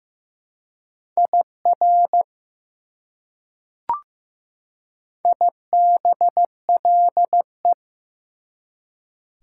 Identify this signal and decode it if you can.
Morse code, heard as a keyed tone, about 15 wpm, 700 Hz. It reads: IR IBLE
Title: Prefixes and Suffixes 15wpm